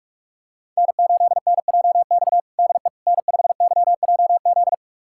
Morse code, heard as a keyed tone, about 41 wpm, 700 Hz. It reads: N8NJX DE N5YJZ
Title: USA Callsign DE Callsign - 5 Characters - 45wpm